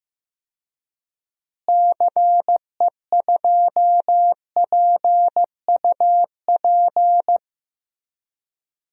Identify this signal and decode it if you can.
Morse code, heard as a keyed tone, about 16 wpm, 700 Hz. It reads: CE2PUP
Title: DXCC-100 Call Signs - 6 Characters - Repeated 3x - 15wpm